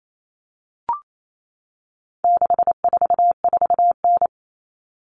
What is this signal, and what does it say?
Morse code, heard as a keyed tone, about 28 wpm, 700 Hz. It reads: 644N
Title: All Cut Numbers - 4-Digit 28wpm